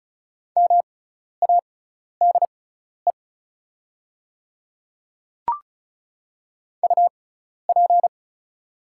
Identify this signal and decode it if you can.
Morse code, heard as a keyed tone, about 36 wpm, spaced as for 12 wpm, 700 Hz. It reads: MADE UP
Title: 6x Character Spacing - Top 500 Words - 35wpm